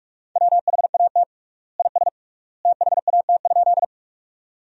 Morse code, heard as a keyed tone, about 46 wpm, 700 Hz. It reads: WHAT IS THAT?
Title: 3x Word-Spacing - Sentences from Top 100 Words - 45wpm